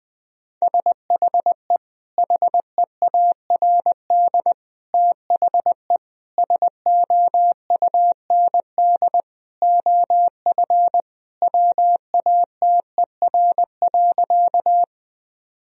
Morse code, heard as a keyed tone, about 20 wpm, 700 Hz. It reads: SHE HEARD THE SOUND OF WATER.
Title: Sentences from Top 1000 Words - Review Part 1 - 20wpm